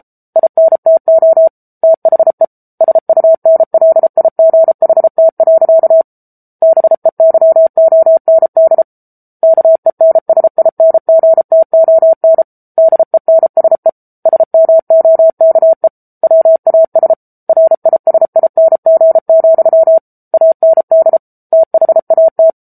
Morse code, heard as a keyed tone, about 34 wpm, 650 Hz. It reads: INTO THE SUNLIGHT. BEYOND KENSINGTON DENSE SMOKE WAS RISING, AND THAT
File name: Book Two / 07-Chapter 7 - 33 WPM